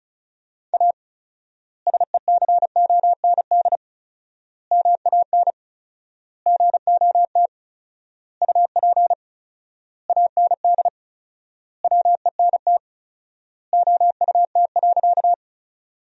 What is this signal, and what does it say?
Morse code, heard as a keyed tone, about 36 wpm, 700 Hz. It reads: A SECOND MAN GOT UP AND WENT OUT.
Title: 4x Word-Spacing - Sentences from Top 300 Words - 35wpm